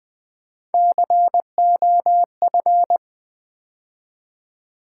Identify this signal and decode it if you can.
Morse code, heard as a keyed tone, about 20 wpm, 690 Hz. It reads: COF
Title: Build A Word - 3 Letters At A Time - 6-Letter Words - 20wpm